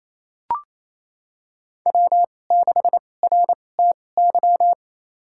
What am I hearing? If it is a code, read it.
Morse code, heard as a keyed tone, about 28 wpm, spaced as for 20 wpm, 700 Hz. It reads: W6RTY
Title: USA Call Signs - 2x Character Spacing - 28wpm